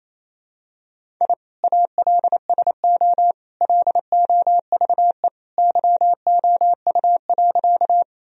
Morse code, heard as a keyed tone, about 28 wpm, 700 Hz. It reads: I ALSO LOVE YOU.
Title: Sentences from Top 1000 Words - Review Part 1 - 28wpm